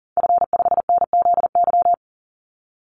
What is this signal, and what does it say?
Morse code, heard as a keyed tone, about 40 wpm, 700 Hz. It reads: F5NZY
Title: Call Signs - 40wpm